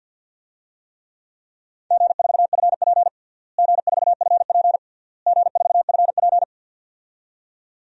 Morse code, heard as a keyed tone, about 51 wpm, 700 Hz. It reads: G4FP G4FP G4FP
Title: DXCC-100 Call Signs - 4 Characters - Repeated 3x - 50wpm